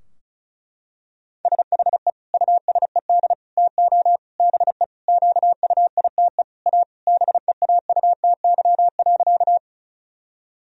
Morse code, heard as a keyed tone, about 34 wpm, 700 Hz. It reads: SHE USED TO BE QUITE A BEAUTY.
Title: Sentences from Top 1000 Words - Review Part 2 - 35wpm